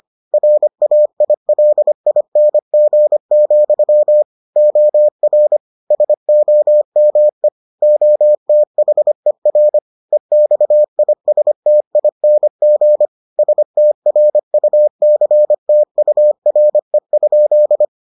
Morse code, heard as a keyed tone, about 25 wpm, 600 Hz. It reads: RAILING, OR SOME OTHER EXISTING STRUCTURE?